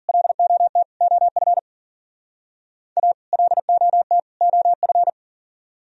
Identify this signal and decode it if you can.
Morse code, heard as a keyed tone, about 52 wpm, 700 Hz. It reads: LOT OF A LOT OF